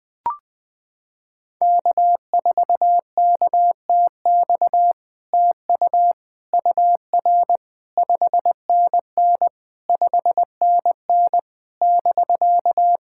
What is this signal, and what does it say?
Morse code, heard as a keyed tone, about 20 wpm, 700 Hz. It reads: K4KTX TU UR 5NN 5NN <BK>
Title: SOTA QSO Practice 20wpm